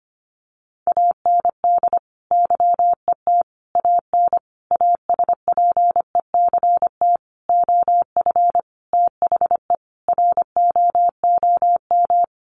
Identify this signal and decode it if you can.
Morse code, heard as a keyed tone, about 25 wpm, 700 Hz. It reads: AND YET AN ASPECT OF THE ROOM
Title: Sentences from Top 1300 Words 25wpm